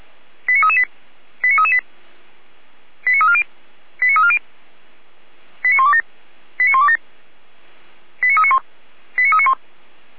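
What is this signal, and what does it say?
Five-tone selective call (ZVEI), heard as quick runs of five tones, tones of about 70 ms. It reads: 89338 89338 89340 89340 89127 89127 89391 89391
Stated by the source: ZVEI five-tone selective call received on SDR.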